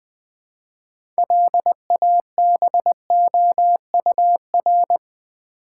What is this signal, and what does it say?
Morse code, heard as a keyed tone, about 20 wpm, 700 Hz. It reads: LABOUR
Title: 6 Letter Words - First 3 Letters Given - 20wpm